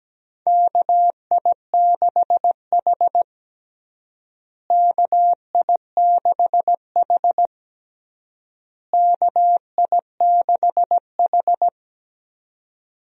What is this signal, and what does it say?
Morse code, heard as a keyed tone, about 17 wpm, 700 Hz. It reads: KI6H KI6H KI6H
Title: USA Call Signs - 4 Characters - Repeated 3x - 17wpm